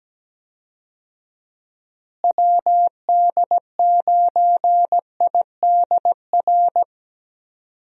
Morse code, heard as a keyed tone, about 17 wpm, 700 Hz. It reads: WD9IDR
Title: USA Call Signs - 6 Characters - Repeated 2x - 17wpm